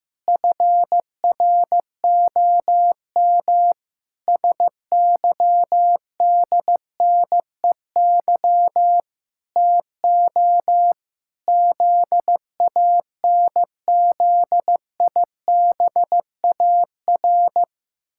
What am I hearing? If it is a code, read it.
Morse code, heard as a keyed tone, about 15 wpm, 700 Hz. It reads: FROM SYDNEY TO ZANZIBAR
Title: Pangram Challenge v2 - 15wpm